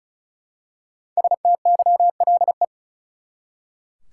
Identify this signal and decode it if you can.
Morse code, heard as a keyed tone, about 36 wpm, 700 Hz. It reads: STYLE
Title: Five Letter Words - 35wpm